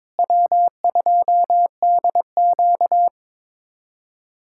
Morse code, heard as a keyed tone, about 22 wpm, 700 Hz. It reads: W2DQ
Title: USA Callsign DE Callsign - 4 Characters - 22wpm